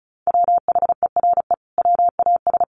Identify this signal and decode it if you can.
Morse code, heard as a keyed tone, about 34 wpm, 700 Hz. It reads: WHERE WAS
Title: Sentences from Top 1500 Words 35wpm